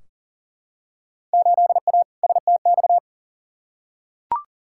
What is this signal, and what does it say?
Morse code, heard as a keyed tone, about 40 wpm, 700 Hz. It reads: 8A STX
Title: Field Day Exchanges Encore - 40wpm